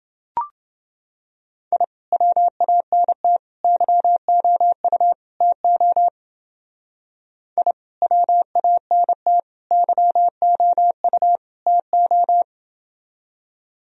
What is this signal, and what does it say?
Morse code, heard as a keyed tone, about 29 wpm, 700 Hz. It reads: I WANT YOU TO I WANT YOU TO